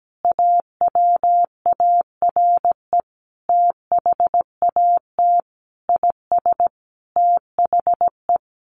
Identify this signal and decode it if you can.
Morse code, heard as a keyed tone, about 17 wpm, 700 Hz. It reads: AWARE THAT IS THE
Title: Sentences from Top 1200 Words 17wpm